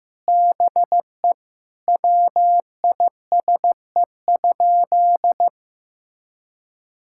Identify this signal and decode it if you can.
Morse code, heard as a keyed tone, about 15 wpm, 700 Hz. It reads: BE WISE?
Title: Sentences from Top 1600 Words 15wpm